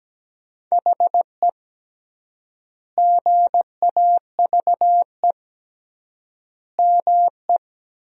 Morse code, heard as a keyed tone, about 17 wpm, 700 Hz. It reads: HE GAVE ME
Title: 3x Word-Spacing - Sentences from Top 500 Words - 17wpm